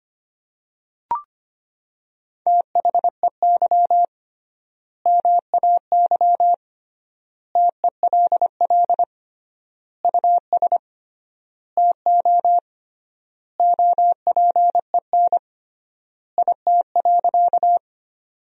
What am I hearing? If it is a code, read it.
Morse code, heard as a keyed tone, about 25 wpm, 700 Hz. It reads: THEY MAY TELL US TO OPEN IT.
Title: 3x Word-Spacing - Sentences from Top 300 Words - 25wpm